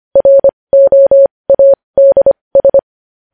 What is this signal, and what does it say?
Morse code, heard as a keyed tone, about 25 wpm, spaced as for 20 wpm, 550 Hz. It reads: ROADS
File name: podcast / News headlines-1x-25x20